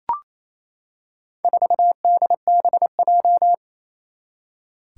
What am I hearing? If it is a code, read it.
Morse code, heard as a keyed tone, about 28 wpm, 700 Hz. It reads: 4DBJ
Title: Four Challenging Letter-Number Combinations 28wpm